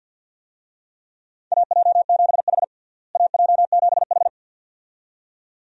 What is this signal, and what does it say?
Morse code, heard as a keyed tone, about 48 wpm, 700 Hz. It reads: AJ7H AJ7H
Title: USA Call Signs - 4 Characters - Repeated 2x - 50wpm